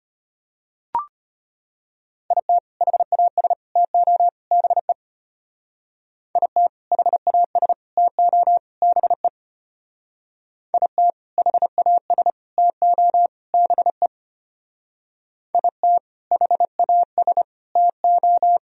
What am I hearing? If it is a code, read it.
Morse code, heard as a keyed tone, about 30 wpm, 700 Hz. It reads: IT HAS TO BE IT HAS TO BE IT HAS TO BE IT HAS TO
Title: Speed-Racing - Sets of 4 Words 25wpm